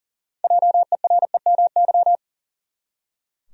Morse code, heard as a keyed tone, about 40 wpm, 700 Hz. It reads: JEREMY
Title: USA Common Names - 40wpm